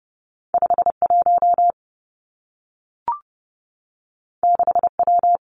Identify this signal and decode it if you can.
Morse code, heard as a keyed tone, about 30 wpm, 700 Hz. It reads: 5 1 6 W
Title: Two Challenging Letter-Number Combinations 30wpm